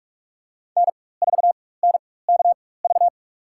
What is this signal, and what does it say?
Morse code, heard as a keyed tone, about 47 wpm, spaced as for 21 wpm, 700 Hz. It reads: N4NKV
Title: USA Call Signs - 4x Character Spacing - 45wpm